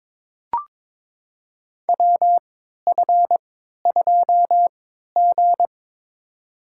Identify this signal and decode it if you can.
Morse code, heard as a keyed tone, about 22 wpm, spaced as for 12 wpm, 700 Hz. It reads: WF2G
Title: USA Call Signs - 4 Characters - 3x Character Spacing - 22wpm